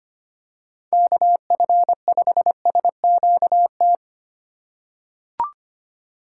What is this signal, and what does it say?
Morse code, heard as a keyed tone, about 25 wpm, 700 Hz. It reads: KF5SQT
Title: USA Call Signs - 6 Characters - 2x Character Spacing - 25wpm